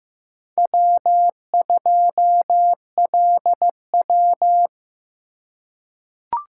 Morse code, heard as a keyed tone, about 15 wpm, 700 Hz. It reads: W2LW
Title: USA Call Signs - 4 Characters - 3x Character Spacing - 15wpm